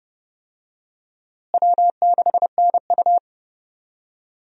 Morse code, heard as a keyed tone, about 30 wpm, 700 Hz. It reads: W6NU
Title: USA Call Signs - 4 Characters - Repeated 3x - 30wpm